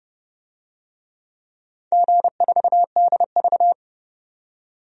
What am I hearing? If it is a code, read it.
Morse code, heard as a keyed tone, about 30 wpm, 700 Hz. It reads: G4DV
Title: DXCC-100 Call Signs - 4 Characters - 30wpm